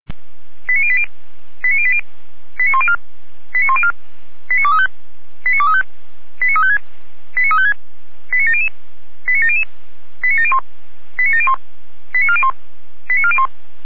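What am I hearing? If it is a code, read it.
Five-tone selective call (ZVEI), heard as quick runs of five tones, tones of about 70 ms. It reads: 89080 89080 89104 89104 89235 89235 89356 89356 89700 89700 89701 89701 89401 89401
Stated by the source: ZVEI five-tone selective call received by SDR.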